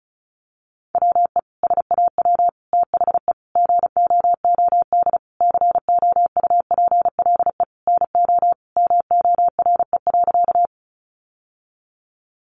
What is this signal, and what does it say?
Morse code, heard as a keyed tone, about 34 wpm, 700 Hz. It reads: WE SAW THE GOOD COUPLE NO MORE.
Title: Sentences from Top 1100 Words 35wpm